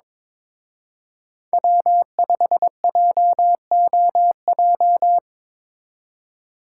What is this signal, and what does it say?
Morse code, heard as a keyed tone, about 25 wpm, 700 Hz. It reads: W5JOJ
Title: Speed Racing - USA 1x3 Call Signs 22wpm